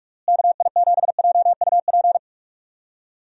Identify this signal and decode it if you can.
Morse code, heard as a keyed tone, about 46 wpm, 700 Hz. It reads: KI7JUP
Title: USA Call Signs - 6 Characters - Repeated 2x - 45wpm